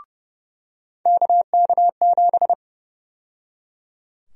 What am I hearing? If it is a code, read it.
Morse code, heard as a keyed tone, about 30 wpm, 700 Hz. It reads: KK7
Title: USA Call Sign Prefixes - 30wpm